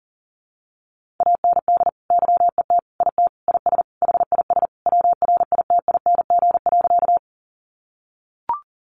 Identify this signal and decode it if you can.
Morse code, heard as a keyed tone, about 40 wpm, 700 Hz. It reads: AND YET IT IS HIS WRITING.
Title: Sentences from Top 800 Words 40wpm